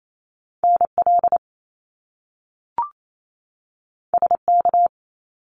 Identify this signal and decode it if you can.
Morse code, heard as a keyed tone, about 28 wpm, 700 Hz. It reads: NL SK
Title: Canadian Provinces and Territories 28wpm